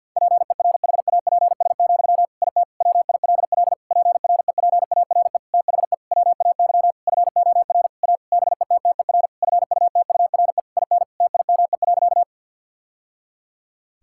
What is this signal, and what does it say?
Morse code, heard as a keyed tone, about 50 wpm, 700 Hz. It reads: PERHAPS, IT WILL PREPARE THE WAY FOR A BETTER FUTURE IN TIME.
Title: Sentences from Top 1000 Words - Review Part 6 - 50wpm